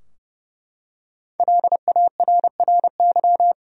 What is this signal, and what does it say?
Morse code, heard as a keyed tone, about 30 wpm, 700 Hz. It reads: LARRY
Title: USA Common Names - 30wpm